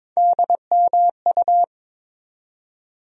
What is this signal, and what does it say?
Morse code, heard as a keyed tone, about 21 wpm, 700 Hz. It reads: DMU
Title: USA Call Sign Postfixes - 3 Letters 22wpm